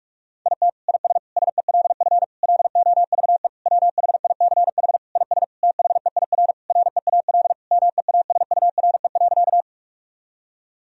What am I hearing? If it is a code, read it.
Morse code, heard as a keyed tone, about 46 wpm, 700 Hz. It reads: IT IS SELF LOVE WHICH IS THEIR REAL MEASURE.
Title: Sentences from Top 1000 Words - Review Part 1 - 45wpm